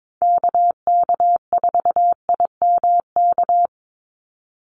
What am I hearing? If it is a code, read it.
Morse code, heard as a keyed tone, about 22 wpm, 700 Hz. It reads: KK4IMK
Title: USA Call Signs 22wpm